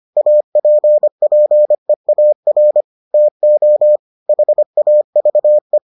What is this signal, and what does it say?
Morse code, heard as a keyed tone, about 25 wpm, 600 Hz. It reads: APPEAR TO HAVE